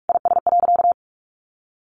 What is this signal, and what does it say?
Morse code, heard as a keyed tone, about 44 wpm, 700 Hz. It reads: IS.
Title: Sentences from Top 1100 Words 45wpm